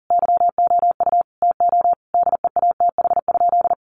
Morse code, heard as a keyed tone, about 40 wpm, 700 Hz. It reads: YOU TO DEATH?
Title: Sentences from Top 1100 Words 40wpm